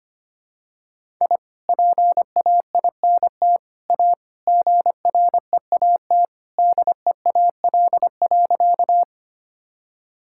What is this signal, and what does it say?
Morse code, heard as a keyed tone, about 25 wpm, 700 Hz. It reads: I PAINT A GREAT DEAL.
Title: Sentences from Top 1000 Words - Review Part 6 - 25wpm